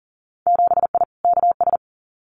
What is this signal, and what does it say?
Morse code, heard as a keyed tone, about 40 wpm, 700 Hz. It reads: 7I KS
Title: Winter Field Day Exchanges Encore 40wpm